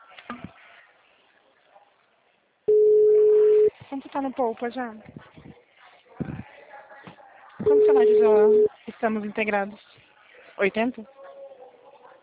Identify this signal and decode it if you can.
Ringback tone (Europe, 425 Hz), with speech.
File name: out- -2016- - - .59997